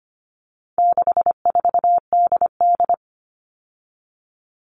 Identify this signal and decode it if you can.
Morse code, heard as a keyed tone, about 25 wpm, 700 Hz. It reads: 64DD
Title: All Cut Numbers - 4-Digit 25wpm